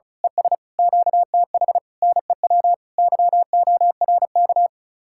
Morse code, heard as a keyed tone, about 32 wpm, 700 Hz. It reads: ES QTH NEW YORK
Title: Speed-Racing Common QSO Phrases - 35wpm